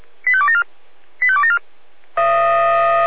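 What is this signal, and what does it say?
Five-tone selective call (ZVEI), heard as quick runs of five tones, tones of about 70 ms. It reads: 85274 85274
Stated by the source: ZVEI five-tone selective call 85830 received on an SDR.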